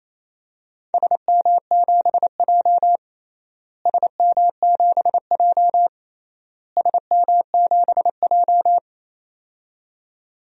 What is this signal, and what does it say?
Morse code, heard as a keyed tone, about 28 wpm, 700 Hz. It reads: SM7J SM7J SM7J
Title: DXCC-100 Call Signs - 4 Characters - Repeated 3x - 28wpm